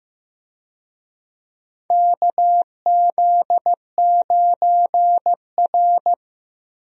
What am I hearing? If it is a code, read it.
Morse code, heard as a keyed tone, about 15 wpm, 700 Hz. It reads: KZ9R
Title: USA Call Signs - 4 Characters - Repeated 3x - 15wpm